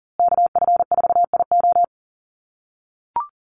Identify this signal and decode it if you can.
Morse code, heard as a keyed tone, about 40 wpm, 700 Hz. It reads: KF4IO
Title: USA Call Signs Repeated 2x 40wpm